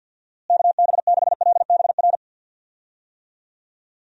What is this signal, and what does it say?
Morse code, heard as a keyed tone, about 47 wpm, 700 Hz. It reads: KB6LBR
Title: USA Call Signs - 6 Characters - Repeated 2x - 50wpm